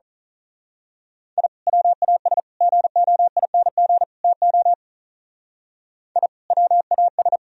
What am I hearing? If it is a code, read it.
Morse code, heard as a keyed tone, about 38 wpm, 700 Hz. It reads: I WAS GOING TO I WAS
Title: Speed-Racing - Sets of 4 Words 35wpm